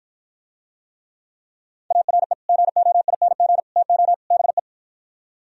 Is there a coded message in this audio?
Yes: Morse code from 1.9 to 4.6 s, heard as a keyed tone, about 56 wpm, 700 Hz. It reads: ARE GOING TO BE